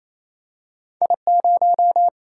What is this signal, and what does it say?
Morse code, heard as a keyed tone, about 28 wpm, 700 Hz. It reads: I 0
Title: Single-Letter Single-Number 28wpm